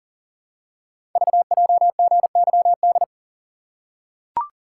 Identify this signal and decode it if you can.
Morse code, heard as a keyed tone, about 40 wpm, 700 Hz. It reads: VJGYD
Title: Code Groups - 5 Letters - 40wpm